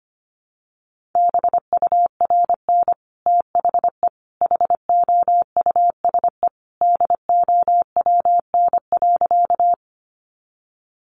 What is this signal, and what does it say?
Morse code, heard as a keyed tone, about 25 wpm, 700 Hz. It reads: BURN THE HOUSE DOWN.
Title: Sentences from Top 1300 Words 25wpm